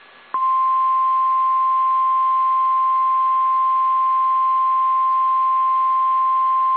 Emergency Alert System (SAME) tones (the 1050 Hz attention tone). It is heard from 0.6 to 5.8 s.
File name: eas / KSDZKDJL / KSDZKDJL